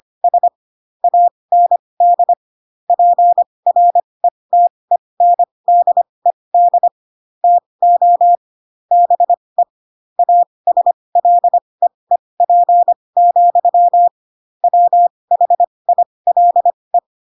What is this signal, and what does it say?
Morse code, heard as a keyed tone, about 25 wpm, spaced as for 20 wpm, 700 Hz. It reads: S AND PRETENDED TO BE ASLEEP, WHILE
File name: Pinocchio / Pinocchio ch8 15 25 ik1hge